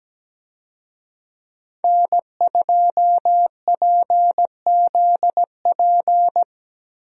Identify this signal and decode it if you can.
Morse code, heard as a keyed tone, about 17 wpm, 700 Hz. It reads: N2PZP
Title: USA Call Signs - 5 Characters - Repeated 2x - 17wpm